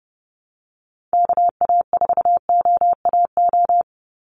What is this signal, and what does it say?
Morse code, heard as a keyed tone, about 30 wpm, 700 Hz. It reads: KA4OAO
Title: USA Call Signs 30wpm